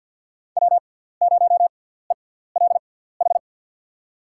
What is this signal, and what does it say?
Morse code, heard as a keyed tone, about 48 wpm, spaced as for 17 wpm, 690 Hz. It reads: W0ELH
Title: USA Call Signs - 6x Character Spacing - 50wpm